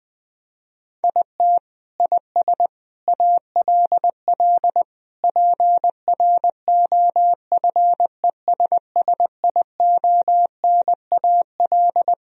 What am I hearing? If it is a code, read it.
Morse code, heard as a keyed tone, about 20 wpm, 700 Hz. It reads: IT IS ALL PROFESSIONAL
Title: Sentences from Top 1000 Words - Review Part 4 - 20wpm